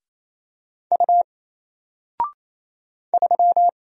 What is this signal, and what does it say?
Morse code, heard as a keyed tone, about 27 wpm, 700 Hz. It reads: U 3
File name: Single-Letter-Number-28wpm